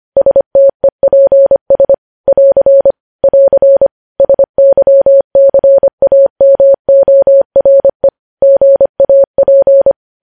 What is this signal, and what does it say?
Morse code, heard as a keyed tone, about 25 wpm, 550 Hz. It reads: STEPS <AR> <AR> SYCAMORE GAP
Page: news headlines in morse code at 25 wpm